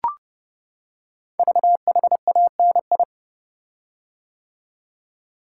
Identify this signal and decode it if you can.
Morse code, heard as a keyed tone, about 31 wpm, 690 Hz. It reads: VHANI
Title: Code Groups - 5 Letters - 30wpm